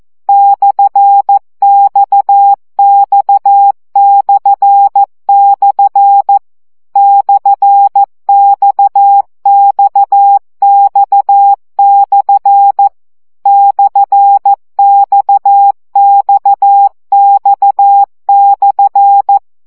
Morse code, heard as a keyed tone, about 14 wpm, 800 Hz. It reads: /XX// /XXX/ /XXX/
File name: CW gyakorlas x per 12 baud